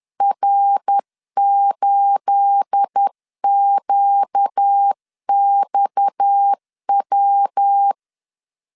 Morse code, heard as a keyed tone, about 10 wpm, 800 Hz. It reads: R8QXW
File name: LEZIONI / 9 28 10wpm 2842